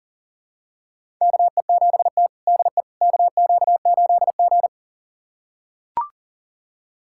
Morse code, heard as a keyed tone, about 40 wpm, 700 Hz. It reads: KE7T DE KQ8G
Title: USA Callsign DE Callsign - 4 Characters - 40wpm